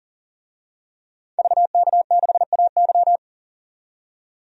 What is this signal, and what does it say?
Morse code, heard as a keyed tone, about 40 wpm, 700 Hz. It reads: VK6AY
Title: DXCC-100 Call Signs - 5 Characters - 40wpm